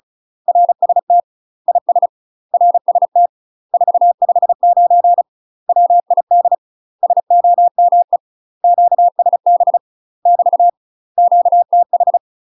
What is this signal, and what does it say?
Morse code, heard as a keyed tone, about 36 wpm, 700 Hz. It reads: RST IS RST 459 WID SOME QSB <BT> QTH
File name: UK MASTER QSOs TRAINING / 35WPM / UK MASTER QSOs 0014